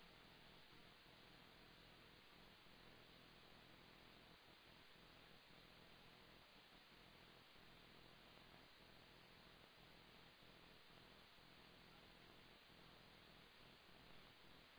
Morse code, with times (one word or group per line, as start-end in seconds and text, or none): none